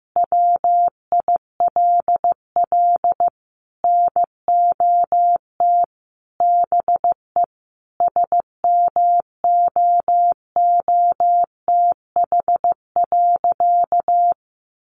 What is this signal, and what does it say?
Morse code, heard as a keyed tone, about 15 wpm, 700 Hz. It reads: WILL NOT BE SMOOTH.
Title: Sentences from Top 1400 Words 15wpm